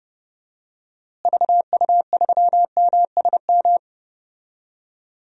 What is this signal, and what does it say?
Morse code, heard as a keyed tone, about 30 wpm, 700 Hz. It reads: VU3MSM
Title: DXCC-100 Call Signs - 6 Characters - Repeated 3x - 30wpm